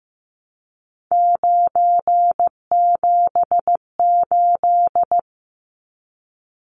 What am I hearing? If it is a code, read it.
Morse code, heard as a keyed tone, about 15 wpm, 700 Hz. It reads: 978
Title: Numbers - 3-Digit 15wpm